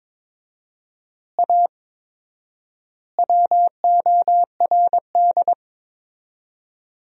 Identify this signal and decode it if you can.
Morse code, heard as a keyed tone, about 22 wpm, 700 Hz. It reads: A WORD
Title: 4x Word-Spacing - Sentences from Top 300 Words - 22wpm